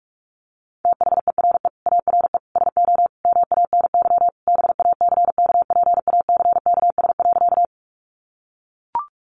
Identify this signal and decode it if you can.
Morse code, heard as a keyed tone, about 44 wpm, 700 Hz. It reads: THERE ARE SO MANY BACKPACKS.
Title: Additional Easy Phrases 45wpm